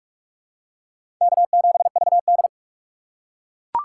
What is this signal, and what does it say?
Morse code, heard as a keyed tone, about 42 wpm, 700 Hz. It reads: K7VD
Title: USA Call Signs - 4 Characters - 4x Character Spacing - 45wpm